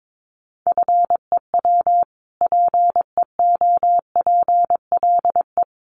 Morse code, heard as a keyed tone, about 22 wpm, 700 Hz. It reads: FEW PEOPLE
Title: Sentences from Top 600 Words 22wpm